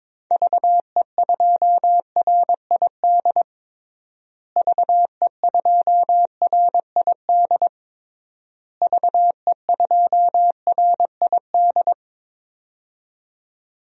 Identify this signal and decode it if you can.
Morse code, heard as a keyed tone, about 22 wpm, 700 Hz. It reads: VE2RID VE2RID VE2RID
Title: DXCC-100 Call Signs - 6 Characters - Repeated 3x - 22wpm